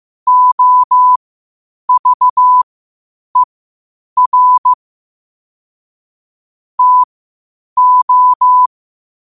Morse code, heard as a keyed tone, about 15 wpm, spaced as for 8 wpm, 1000 Hz. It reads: OVER TO
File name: qotdfiles / 8wpm